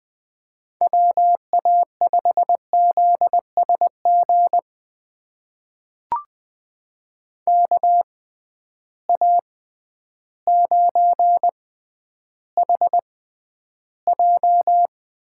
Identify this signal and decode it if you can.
Morse code, heard as a keyed tone, about 20 wpm, spaced as for 11 wpm, 700 Hz. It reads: WA5ZSG KA9HJ
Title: USA Call Signs - 6 Characters - 6x Character Spacing - 20wpm